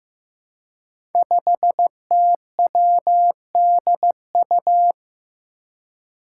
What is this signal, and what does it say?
Morse code, heard as a keyed tone, about 15 wpm, 700 Hz. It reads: 5TWDU
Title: Five Challenging Letter-Number Combinations 15wpm